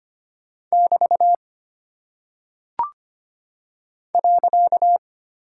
Morse code, heard as a keyed tone, about 25 wpm, 700 Hz. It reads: <BT> .
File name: Single Punctuation-Prosign - 25wpm